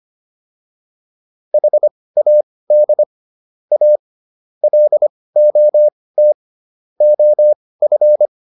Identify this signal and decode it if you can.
Morse code, heard as a keyed tone, about 25 wpm, spaced as for 18 wpm, 600 Hz. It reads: HAD A LOT OF